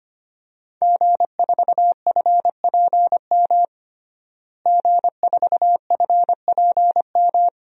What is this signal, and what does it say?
Morse code, heard as a keyed tone, about 25 wpm, 700 Hz. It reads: G4FPM G4FPM
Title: DXCC-100 Call Signs - 5 Characters - Repeated 2x - 25wpm